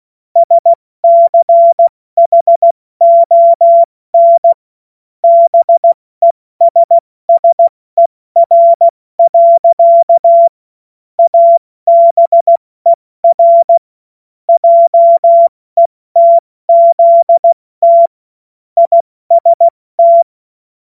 Morse code, heard as a keyed tone, about 16 wpm, 680 Hz. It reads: SCHON BESSER. ABER JETZT IST
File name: Morsekurs / Witze / Witz 21. Dezember 2021, 70 F80